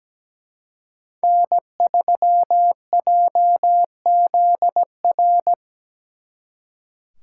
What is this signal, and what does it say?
Morse code, heard as a keyed tone, about 20 wpm, 700 Hz. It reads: N3JZR
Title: Speed Racing - USA 1x3 Call Signs 17wpm